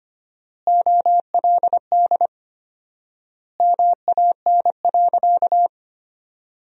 Morse code, heard as a keyed tone, about 25 wpm, 700 Hz. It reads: OLD MAN.
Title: 4x Word-Spacing - Sentences from Top 200 Words - 25wpm